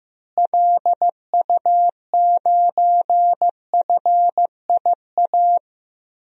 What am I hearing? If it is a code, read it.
Morse code, heard as a keyed tone, about 16 wpm, 700 Hz. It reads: LU9FIA
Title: DXCC-100 Call Signs - 6 Characters - Repeated 3x - 15wpm